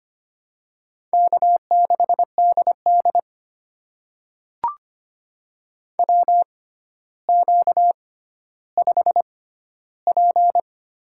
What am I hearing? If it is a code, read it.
Morse code, heard as a keyed tone, about 25 wpm, spaced as for 13 wpm, 700 Hz. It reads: K6DD WQ5P
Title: USA Call Signs - 4 Characters - 6x Character Spacing - 25wpm